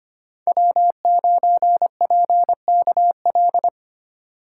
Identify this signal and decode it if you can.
Morse code, heard as a keyed tone, about 25 wpm, 700 Hz. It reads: W9PKL
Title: USA Call Signs - 5 Characters - 2x Character Spacing - 25wpm